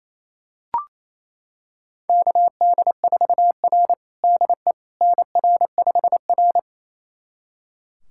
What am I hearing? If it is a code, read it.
Morse code, heard as a keyed tone, about 28 wpm, 700 Hz. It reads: KD4R DE NR5R
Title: USA Callsign DE Callsign - 4 Characters - 28wpm